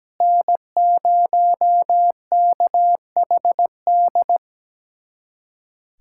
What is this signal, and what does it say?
Morse code, heard as a keyed tone, about 17 wpm, 700 Hz. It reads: N0KHD
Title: USA Callsign DE Callsign - 5 Characters - 17wpm